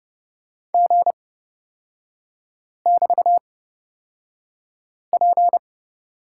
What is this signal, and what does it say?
Morse code, heard as a keyed tone, about 30 wpm, spaced as for 5 wpm, 700 Hz. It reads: G<BT>P
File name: Lesson 36e - BT - Three Characters 30wpm